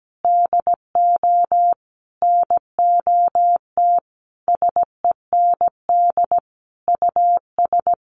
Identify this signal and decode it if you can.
Morse code, heard as a keyed tone, about 17 wpm, 700 Hz. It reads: DO NOT SEND US
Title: Sentences from Top 1500 Words 17wpm